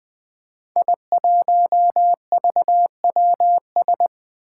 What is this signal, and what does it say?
Morse code, heard as a keyed tone, about 20 wpm, 700 Hz. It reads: I1VWS
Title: Five Challenging Letter-Number Combinations 20wpm